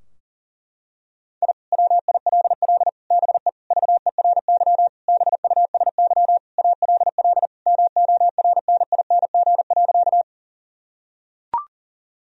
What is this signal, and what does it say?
Morse code, heard as a keyed tone, about 40 wpm, 700 Hz. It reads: I WILL BE VERY BUSY ALL MORNING.
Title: Sentences from Top 1000 Words - Review Part 3 - 40wpm